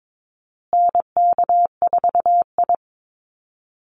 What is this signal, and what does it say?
Morse code, heard as a keyed tone, about 22 wpm, 700 Hz. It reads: NK4I
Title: Call Signs Repeated 3x - 22wpm